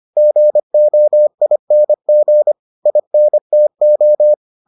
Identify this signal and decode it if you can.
Morse code, heard as a keyed tone, about 25 wpm, 600 Hz. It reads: GOING INTO